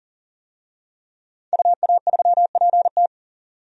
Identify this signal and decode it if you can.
Morse code, heard as a keyed tone, about 40 wpm, 700 Hz. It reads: UA3PT
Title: DXCC-100 Call Signs - 5 Characters - 40wpm